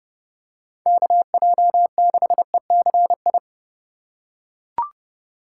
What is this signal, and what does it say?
Morse code, heard as a keyed tone, about 30 wpm, 700 Hz. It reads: KJ6ECI
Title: USA Call Signs - 6 Characters - 4x Character Spacing - 30wpm